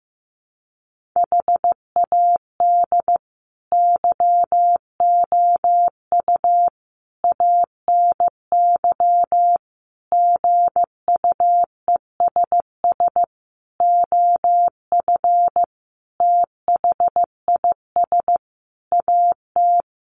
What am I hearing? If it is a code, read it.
Morse code, heard as a keyed tone, about 15 wpm, 700 Hz. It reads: HAD YOU ANY GUESS OF THIS AT
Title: Sentences from Top 1500 Words 15wpm